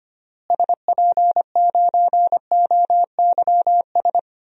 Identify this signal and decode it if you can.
Morse code, heard as a keyed tone, about 25 wpm, 700 Hz. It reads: SP9OYS
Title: DXCC-100 Call Signs - 6 Characters - Repeated 2x - 25wpm